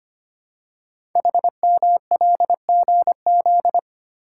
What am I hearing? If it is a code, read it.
Morse code, heard as a keyed tone, about 25 wpm, 700 Hz. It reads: HMLGZ
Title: Code Groups - 5 Letters - 25wpm